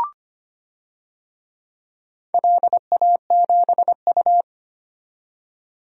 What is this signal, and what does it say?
Morse code, heard as a keyed tone, about 25 wpm, 700 Hz. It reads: LA7U
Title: DXCC-100 Call Signs - 4 Characters - 25wpm